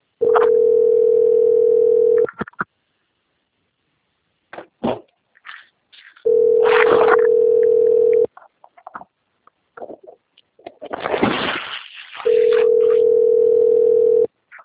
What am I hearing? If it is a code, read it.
Ringback tone (North America), 440 + 480 Hz.